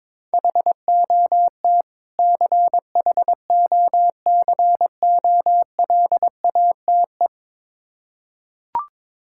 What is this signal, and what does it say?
Morse code, heard as a keyed tone, about 21 wpm, 690 Hz. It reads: HOT CHOCOLATE
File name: Difficult Words with Hints - 22wpm